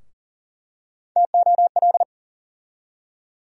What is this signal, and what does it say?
Morse code, heard as a keyed tone, about 40 wpm, 700 Hz. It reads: TOL
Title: USA Call Sign Postfixes - 3 Letters 40wpm